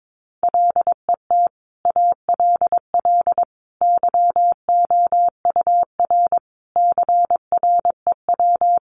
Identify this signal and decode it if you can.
Morse code, heard as a keyed tone, about 22 wpm, 700 Hz. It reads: LET ALL YOUR CREW
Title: Sentences from Top 1500 Words 22wpm